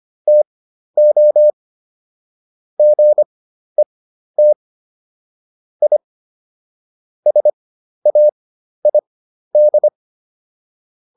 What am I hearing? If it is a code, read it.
Morse code, heard as a keyed tone, about 25 wpm, spaced as for 12 wpm, 600 Hz. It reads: TO GET I SAID